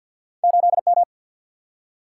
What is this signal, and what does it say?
Morse code, heard as a keyed tone, about 52 wpm, 700 Hz. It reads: 8M
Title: Single-Letter Single-Number 50wpm